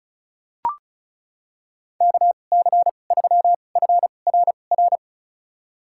Morse code, heard as a keyed tone, about 34 wpm, spaced as for 25 wpm, 700 Hz. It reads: KC3FRR
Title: USA Call Signs - 6 Characters - 2x Character Spacing - 35wpm